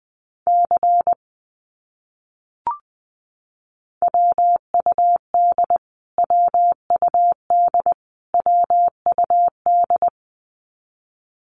Morse code, heard as a keyed tone, about 20 wpm, 700 Hz. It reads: C WUD WUD WUD
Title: Abbreviations Repeated 3x - 20wpm